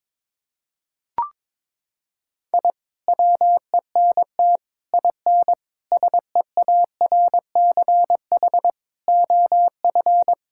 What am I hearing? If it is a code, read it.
Morse code, heard as a keyed tone, about 22 wpm, 700 Hz. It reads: I WENT IN SEARCH OF
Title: Sentences from Top 1000 Words - Review Part 5 - 22wpm